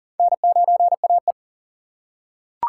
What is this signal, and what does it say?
Morse code, heard as a keyed tone, about 40 wpm, 700 Hz. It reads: N9AE
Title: USA Call Signs - 2x Character Spacing - 40wpm